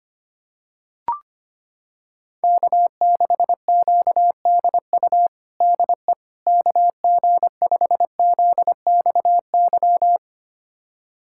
Morse code, heard as a keyed tone, about 25 wpm, 700 Hz. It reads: K6QDU DE KG5ZXY
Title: USA Callsign DE Callsign 25wpm